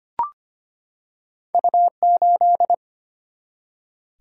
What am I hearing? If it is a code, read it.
Morse code, heard as a keyed tone, about 25 wpm, 700 Hz. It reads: U8
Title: Single-Letter Single-Number 25wpm